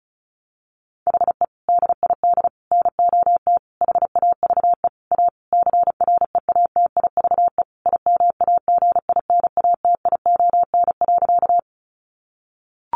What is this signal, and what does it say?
Morse code, heard as a keyed tone, about 34 wpm, 700 Hz. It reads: HE DID NOT HAVE A CREATIVE IMAGINATION.
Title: Sentences from Top 1300 Words 35wpm